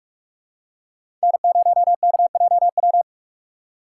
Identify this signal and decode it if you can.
Morse code, heard as a keyed tone, about 47 wpm, 700 Hz. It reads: N0KJW
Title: USA Call Signs - 5 Characters - 3x Character Spacing - 45wpm